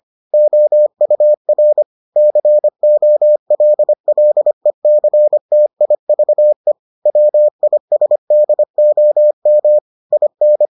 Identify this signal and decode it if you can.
Morse code, heard as a keyed tone, about 25 wpm, 600 Hz. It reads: OUR COLLECTIVE WISDOM IN